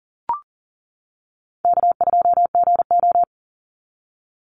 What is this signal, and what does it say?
Morse code, heard as a keyed tone, about 40 wpm, 700 Hz. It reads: K2GO
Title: Call Signs - 40wpm